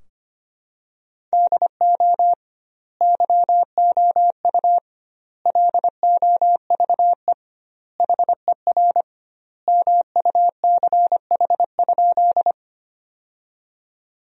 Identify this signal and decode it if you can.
Morse code, heard as a keyed tone, about 25 wpm, 700 Hz. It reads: DO YOU LOVE HER MUCH?
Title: 2x Word-Spacing - Sentences from Top 400 Words - 25wpm